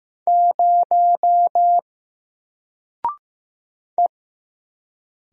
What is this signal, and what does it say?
Morse code, heard as a keyed tone, about 15 wpm, 700 Hz. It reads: 0 E
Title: Single Letter-Number 15wpm